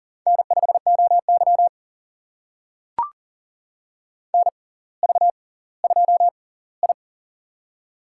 Morse code, heard as a keyed tone, about 40 wpm, spaced as for 20 wpm, 700 Hz. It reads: N5OY NV2I
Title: USA Call Signs - 4 Characters - 6x Character Spacing - 40wpm